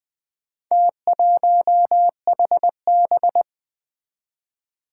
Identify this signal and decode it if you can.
Morse code, heard as a keyed tone, about 20 wpm, 700 Hz. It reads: T1HB
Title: Four Challenging Letter-Number Combinations 20wpm